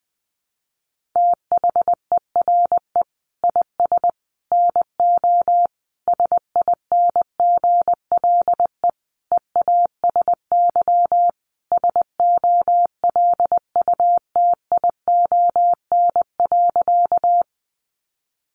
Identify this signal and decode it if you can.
Morse code, heard as a keyed tone, about 20 wpm, 700 Hz. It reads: THERE IS NO SINGLE EASY SOLUTION.
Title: Sentences from Top 1300 Words 20wpm